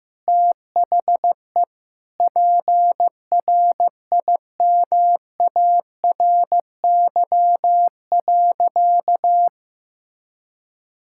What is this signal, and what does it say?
Morse code, heard as a keyed tone, about 15 wpm, 700 Hz. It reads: THE PRIMARY.
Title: Sentences from Top 1000 Words - Review Part 5 - 15wpm